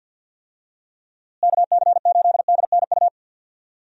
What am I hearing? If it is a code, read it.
Morse code, heard as a keyed tone, about 50 wpm, 700 Hz. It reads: KC8DNU
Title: USA Call Signs - 6 Characters - Repeated 3x - 50wpm